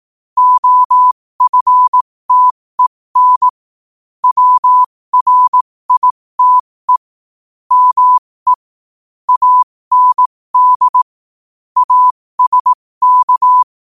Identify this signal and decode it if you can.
Morse code, heard as a keyed tone, about 18 wpm, spaced as for 15 wpm, 1000 Hz. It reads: OFTEN WRITE ME AND ASK
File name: qotdfiles / 15wpm